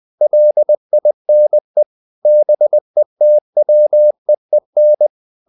Morse code, heard as a keyed tone, about 20 wpm, 600 Hz. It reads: LINE BETWEEN
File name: the old man and the sea / the old man and the sea part 1 20wpm